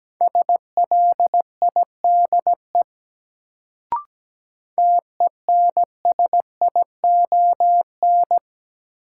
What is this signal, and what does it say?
Morse code, heard as a keyed tone, about 17 wpm, 700 Hz. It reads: SLIDE TENSION
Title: Sentences from Top 1600 Words - New Word Drill Alt - 17wpm